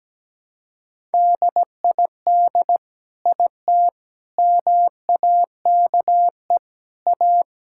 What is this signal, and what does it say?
Morse code, heard as a keyed tone, about 17 wpm, 700 Hz. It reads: DID IT MAKE A
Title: Sentences from Top 1000 Words - Review Part 5 - 17wpm